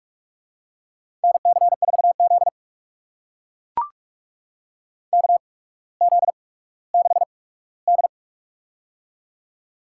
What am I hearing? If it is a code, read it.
Morse code, heard as a keyed tone, about 44 wpm, spaced as for 19 wpm, 700 Hz. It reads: NC4Z KZ6D
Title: USA Call Signs - 4 Characters - 8x Character Spacing - 45wpm